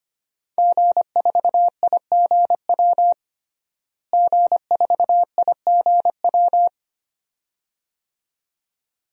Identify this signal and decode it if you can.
Morse code, heard as a keyed tone, about 25 wpm, 700 Hz. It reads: G4IGW G4IGW
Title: DXCC-100 Call Signs - 5 Characters - Repeated 2x - 25wpm